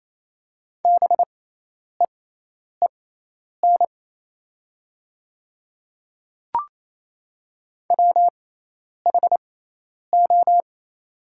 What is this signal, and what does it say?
Morse code, heard as a keyed tone, about 28 wpm, spaced as for 10 wpm, 700 Hz. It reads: BEEN WHO
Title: 6x Character Spacing - Top 100 Words - 28wpm